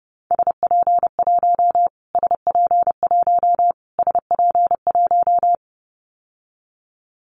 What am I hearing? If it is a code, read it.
Morse code, heard as a keyed tone, about 30 wpm, 690 Hz. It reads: SP1 SP1 SP1
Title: Callsign Prefixes for Top 25 DXCC - Repeated 3x - 30wpm